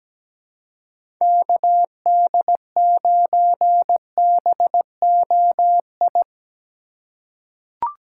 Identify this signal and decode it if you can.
Morse code, heard as a keyed tone, about 17 wpm, 700 Hz. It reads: KD9BOI
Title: USA Call Signs - 6 Characters - 3x Character Spacing - 17wpm